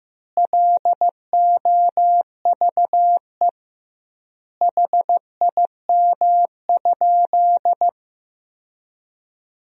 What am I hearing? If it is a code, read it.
Morse code, heard as a keyed tone, about 16 wpm, 700 Hz. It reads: LOVE HIM?
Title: 2x Word-Spacing - Sentences from Top 400 Words - 15wpm